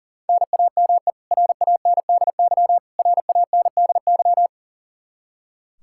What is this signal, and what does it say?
Morse code, heard as a keyed tone, about 40 wpm, 700 Hz. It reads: NAME RANDY RANDY
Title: Practice QSO - Exchange 1 - Beginning - 40wpm